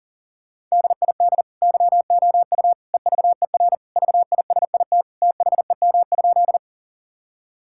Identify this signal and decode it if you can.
Morse code, heard as a keyed tone, about 37 wpm, 700 Hz. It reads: DID YOU EVER VISIT THEM?
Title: Sentences from Top 1000 Words - Review Part 6 - 40wpm